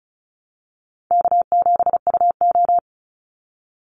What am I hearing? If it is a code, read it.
Morse code, heard as a keyed tone, about 34 wpm, 700 Hz. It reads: K7UO
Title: USA Call Signs 35wpm